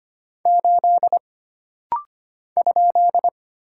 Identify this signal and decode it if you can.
Morse code, heard as a keyed tone, about 25 wpm, 700 Hz. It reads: 8 ?
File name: Single Character (Letter:Number:Punctuation:Prosign) - Rapid-Fire - 25wpm